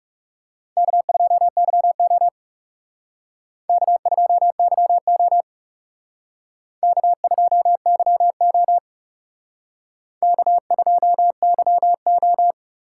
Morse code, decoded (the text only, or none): K2YO K2YO K2YO K2YO